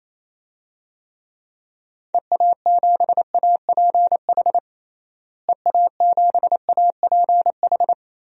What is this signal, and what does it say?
Morse code, heard as a keyed tone, about 28 wpm, 700 Hz. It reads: EA7APH EA7APH
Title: DXCC-100 Call Signs - 6 Characters - Repeated 3x - 28wpm